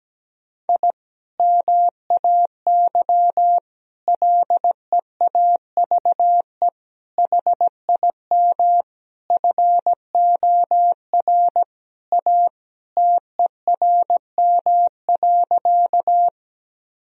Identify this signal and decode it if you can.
Morse code, heard as a keyed tone, about 17 wpm, 700 Hz. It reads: I MAY LEAVE HIM FOR A TERM.
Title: Sentences from Top 1000 Words - Review Part 4 - 17wpm